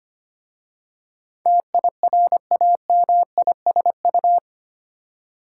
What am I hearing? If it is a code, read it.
Morse code, heard as a keyed tone, about 25 wpm, 700 Hz. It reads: TIRAMISU
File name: practice / 25wpm / Difficult Words with Hints - 25wpm